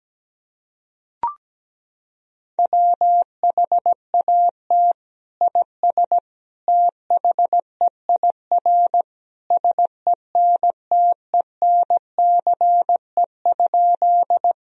Morse code, heard as a keyed tone, about 17 wpm, 700 Hz. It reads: WHAT IS THEIR SENTENCE?
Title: Sentences from Top 1000 Words - Review Part 1 - 17wpm